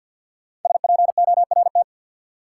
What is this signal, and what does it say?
Morse code, heard as a keyed tone, about 51 wpm, 700 Hz. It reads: SPORT
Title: Sentences from Top 1500 Words - New Word Drill Alt - 50wpm